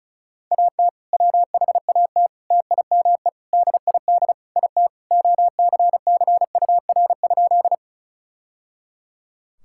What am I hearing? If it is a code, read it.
Morse code, heard as a keyed tone, about 36 wpm, 700 Hz. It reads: AT WHAT TIME DID IT OCCUR?
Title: Sentences from Top 1000 Words - Review Part 6 - 35wpm